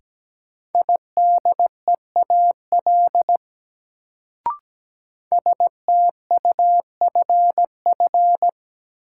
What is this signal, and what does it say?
Morse code, heard as a keyed tone, about 17 wpm, 700 Hz. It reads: IDEAL STUFF
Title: Sentences from Top 1400 Words - New Word Drill - 17wpm